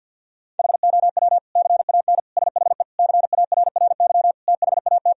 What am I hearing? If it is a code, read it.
Morse code, heard as a keyed tone, about 50 wpm, 700 Hz. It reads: HOW CAN SHE CARRY THAT
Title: Sentences from Top 1000 Words - Review Part 3 - 50wpm